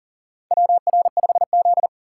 Morse code, heard as a keyed tone, about 40 wpm, 690 Hz. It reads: WR5Z
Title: USA Call Signs - 2x Character Spacing - 40wpm